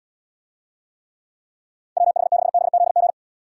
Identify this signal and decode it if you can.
Morse code, heard as a keyed tone, about 78 wpm, 700 Hz. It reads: WH6BZG